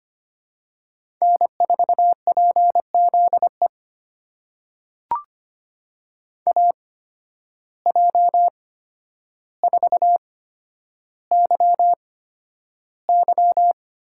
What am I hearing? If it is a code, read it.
Morse code, heard as a keyed tone, about 25 wpm, spaced as for 11 wpm, 700 Hz. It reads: N4PZE AJ4YY
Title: USA Call Signs - 8x Character Spacing - 25wpm